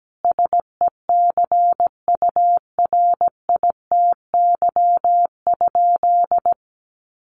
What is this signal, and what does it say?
Morse code, heard as a keyed tone, about 17 wpm, 700 Hz. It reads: SECURITY?
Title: Sentences from Top 1400 Words 17wpm